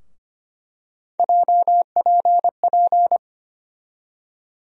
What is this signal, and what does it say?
Morse code, heard as a keyed tone, about 25 wpm, 700 Hz. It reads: JPP
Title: USA Call Sign Postfixes - 3 Letters 25wpm